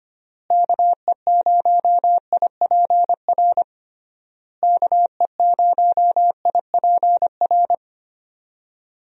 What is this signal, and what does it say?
Morse code, heard as a keyed tone, about 25 wpm, 700 Hz. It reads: KE0IPR KE0IPR
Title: USA Call Signs - 6 Characters - Repeated 3x - 25wpm